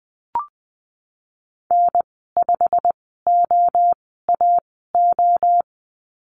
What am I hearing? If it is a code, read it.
Morse code, heard as a keyed tone, about 19 wpm, spaced as for 14 wpm, 700 Hz. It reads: N5OAO
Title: 088 Lesson 15g - Number 5 - Call Signs 20wpm